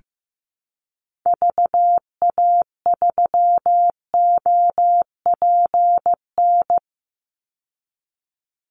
Morse code, heard as a keyed tone, about 15 wpm, 700 Hz. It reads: VA3OPN
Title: Call Signs Repeated 3x - 15wpm